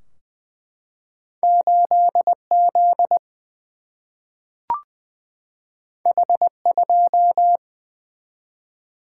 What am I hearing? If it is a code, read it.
Morse code, heard as a keyed tone, about 20 wpm, 700 Hz. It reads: 8 Z H 2
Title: Single-Letter Single-Number 20wpm